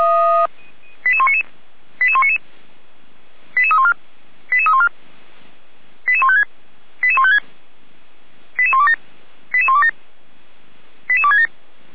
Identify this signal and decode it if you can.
Five-tone selective call (ZVEI), heard as quick runs of five tones, tones of about 70 ms. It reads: 88199 88199 88314 88314 88156 88156 88127 88127 88267